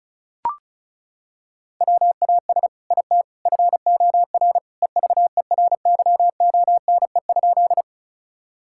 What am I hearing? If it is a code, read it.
Morse code, heard as a keyed tone, about 36 wpm, 700 Hz. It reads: WAS IT FOR EVERYONE?
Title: Sentences from Top 1000 Words - Review Part 5 - 35wpm